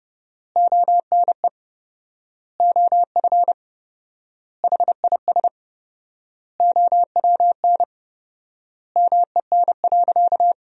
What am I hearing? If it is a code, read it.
Morse code, heard as a keyed tone, about 30 wpm, 700 Hz. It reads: ONE OF HIS OWN MEN.
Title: 4x Word-Spacing - Sentences from Top 300 Words - 30wpm